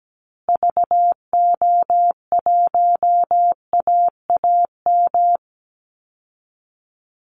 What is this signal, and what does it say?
Morse code, heard as a keyed tone, about 17 wpm, 700 Hz. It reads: VO1AAM
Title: Call Signs Repeated 3x - 17wpm